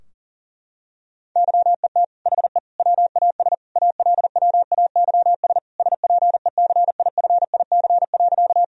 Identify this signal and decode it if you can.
Morse code, heard as a keyed tone, about 37 wpm, 700 Hz. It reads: YET HE WAS ALWAYS SPECIFIC.
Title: Sentences from Top 1000 Words - Review Part 3 - 40wpm